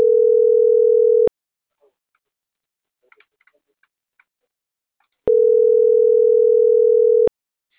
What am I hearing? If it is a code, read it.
Ringback tone (North America), 440 + 480 Hz.